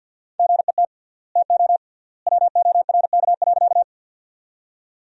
Morse code, heard as a keyed tone, about 50 wpm, 700 Hz. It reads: GET TO WORK.
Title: 3x Word-Spacing - Sentences from Top 200 Words - 50wpm